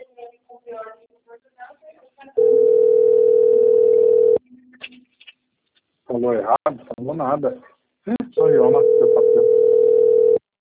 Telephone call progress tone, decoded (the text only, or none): ringback tone